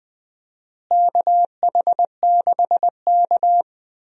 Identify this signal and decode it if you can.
Morse code, heard as a keyed tone, about 20 wpm, 700 Hz. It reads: KH6K
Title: DXCC-100 Call Signs - 4 Characters - Repeated 3x - 20wpm